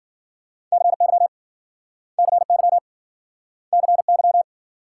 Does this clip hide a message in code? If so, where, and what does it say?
Yes: Morse code from 0.7 to 4.4 s, heard as a keyed tone, about 53 wpm, 700 Hz. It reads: CY CY CY